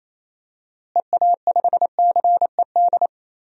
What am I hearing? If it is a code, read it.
Morse code, heard as a keyed tone, about 28 wpm, 700 Hz. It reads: EA5CED
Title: DXCC-100 Call Signs - 6 Characters - Repeated 3x - 28wpm